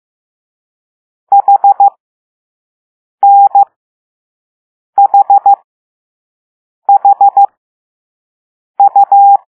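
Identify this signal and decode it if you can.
Morse code, heard as a keyed tone, about 15 wpm, spaced as for 5 wpm, 800 Hz. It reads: HNHHU
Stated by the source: Morse code lesson 5 drill of random five-letter groups using all letters learned so far.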